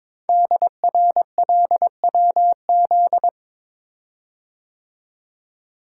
Morse code, heard as a keyed tone, about 23 wpm, 700 Hz. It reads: DRLWZ
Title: Code Groups - 5 Letters - 22wpm